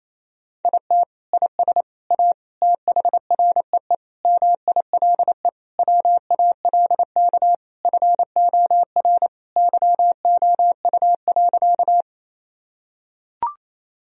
Morse code, heard as a keyed tone, about 28 wpm, 700 Hz. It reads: IT IS A THREE MILE WALK FOR YOU.
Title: Sentences from Top 1000 Words - Review Part 1 - 28wpm